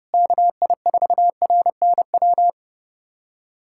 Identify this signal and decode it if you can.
Morse code, heard as a keyed tone, about 30 wpm, 700 Hz. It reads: KI4RNW
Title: USA Callsign DE Callsign - 6 Characters - 30wpm